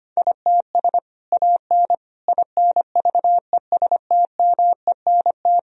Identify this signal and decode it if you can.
Morse code, heard as a keyed tone, about 25 wpm, 700 Hz. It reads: ITS AN INVESTMENT
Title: Sentences from Top 1000 Words - Review Part 4 - 25wpm